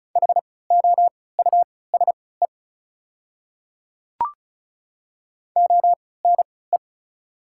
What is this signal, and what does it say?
Morse code, heard as a keyed tone, about 36 wpm, spaced as for 20 wpm, 690 Hz. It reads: HOUSE ONE
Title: 3x Character Spacing - Top 200 Words - 35wpm